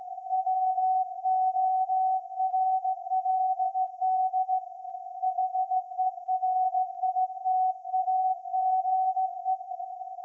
Morse code, heard as a keyed tone, about 16 wpm, 750 Hz. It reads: WORLD HERITAGE
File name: iz7slz / VLF / SAQ 30-06-2024 0900